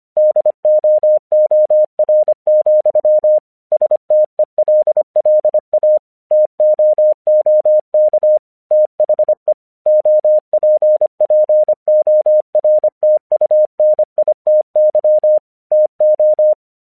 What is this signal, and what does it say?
Morse code, heard as a keyed tone, about 24 wpm, 610 Hz. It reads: DOOR, STELLA TOOK THE OPPORTUNITY TO
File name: 1 English 25wpm / AQC17